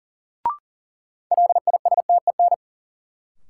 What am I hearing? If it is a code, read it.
Morse code, heard as a keyed tone, about 40 wpm, 700 Hz. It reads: LISTEN
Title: Sentences from Top 400 Words - New Word Drill - 40wpm